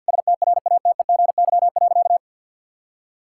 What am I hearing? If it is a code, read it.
Morse code, heard as a keyed tone, about 48 wpm, 690 Hz. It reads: STRATEGY.
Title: Sentences from Top 1000 Words - Review Part 5 - 50wpm